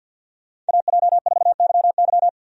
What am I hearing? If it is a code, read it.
Morse code, heard as a keyed tone, about 49 wpm, 700 Hz. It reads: AJ4YY
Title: USA Call Signs - 5 Characters - 50wpm